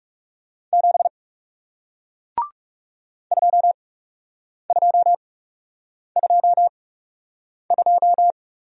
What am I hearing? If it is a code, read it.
Morse code, heard as a keyed tone, about 39 wpm, spaced as for 8 wpm, 700 Hz. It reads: 7 2222